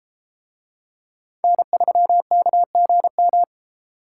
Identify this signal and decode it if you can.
Morse code, heard as a keyed tone, about 33 wpm, 700 Hz. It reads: N3KGM